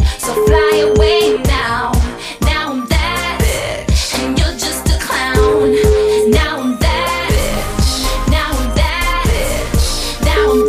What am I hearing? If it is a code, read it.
Ringback tone (Europe, 425 Hz), with noise.